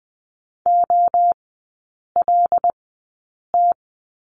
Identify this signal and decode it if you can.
Morse code, heard as a keyed tone, about 20 wpm, spaced as for 8 wpm, 700 Hz. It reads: OLT
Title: 074 Lesson 13e - Letter L - Three Characters 20wpm